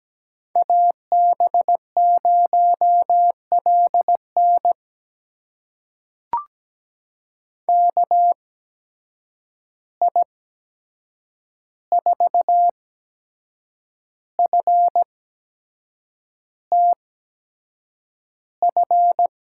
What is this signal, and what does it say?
Morse code, heard as a keyed tone, about 17 wpm, spaced as for 7 wpm, 700 Hz. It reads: AB0LN KI4FTF
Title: USA Call Signs - 8x Character Spacing - 17wpm